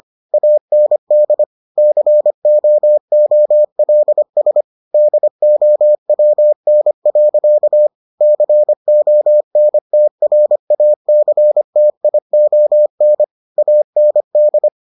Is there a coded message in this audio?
Yes: Morse code from 0.3 to 14.7 s, heard as a keyed tone, about 25 wpm, 600 Hz. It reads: AND COOLS DOWN. CONTRACTION AND